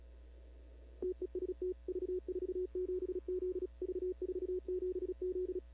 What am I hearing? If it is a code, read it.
Morse code, heard as a keyed tone, about 38 wpm, 410 Hz. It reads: TEST V47Z V47Z
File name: CW / 2005-k5tr arrl cw dx -15:00